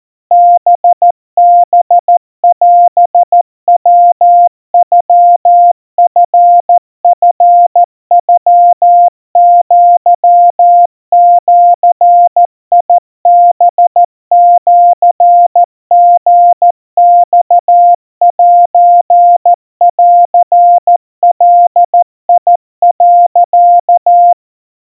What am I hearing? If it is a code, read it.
Morse code (Japanese Wabun code), heard as a keyed tone, about 14 wpm, 700 Hz. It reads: ハハオヤノチチノアジハシリマセンガ、
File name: kawa / Morse sample / kamyu 50 cpm 700 Hz Crispy